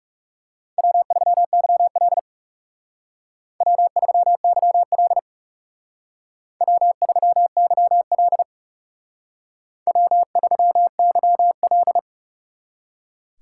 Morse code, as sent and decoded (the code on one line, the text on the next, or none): .-- ...-- -.-- .-.. / .-- ...-- -.-- .-.. / .-- ...-- -.-- .-.. / .-- ...-- -.-- .-..
W3YL W3YL W3YL W3YL